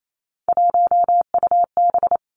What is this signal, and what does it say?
Morse code, heard as a keyed tone, about 28 wpm, 700 Hz. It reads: 1UB
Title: Three Challenging Letter-Number Combinations 28wpm